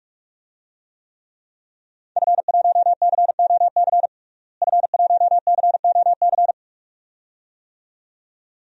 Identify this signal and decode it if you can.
Morse code, heard as a keyed tone, about 46 wpm, 700 Hz. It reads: F1COC F1COC
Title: DXCC-100 Call Signs - 5 Characters - Repeated 2x - 45wpm